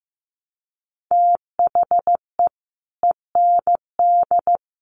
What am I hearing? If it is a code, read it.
Morse code, heard as a keyed tone, about 14 wpm, 700 Hz. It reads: THE END
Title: Sets of 4 Words 15wpm